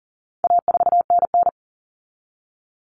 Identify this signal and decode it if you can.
Morse code, heard as a keyed tone, about 37 wpm, 700 Hz. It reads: A4NN
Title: Common Cut Numbers - 4-Digit 40wpm